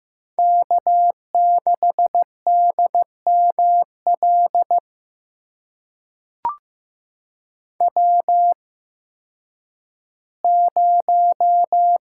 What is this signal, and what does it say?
Morse code, heard as a keyed tone, about 15 wpm, spaced as for 10 wpm, 700 Hz. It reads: K6DML W0
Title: USA Call Signs - 8x Character Spacing - 15wpm